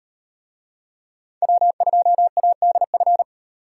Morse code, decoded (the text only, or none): W2ADF